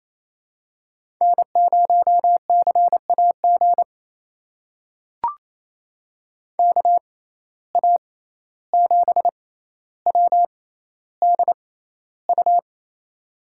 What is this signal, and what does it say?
Morse code, heard as a keyed tone, about 28 wpm, spaced as for 14 wpm, 700 Hz. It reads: N0CAG KA7WDU
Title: USA Call Signs - 6x Character Spacing - 28wpm